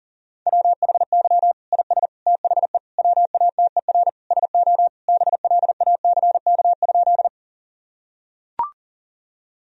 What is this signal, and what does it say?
Morse code, heard as a keyed tone, about 40 wpm, 700 Hz. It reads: WHY IS THE WATER SO BLACK?
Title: Sentences from Top 1000 Words - Review Part 2 - 40wpm